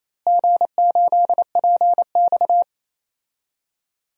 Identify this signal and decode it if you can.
Morse code, heard as a keyed tone, about 28 wpm, 700 Hz. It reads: G8PX
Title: DXCC-100 Call Signs - 4 Characters - Repeated 3x - 28wpm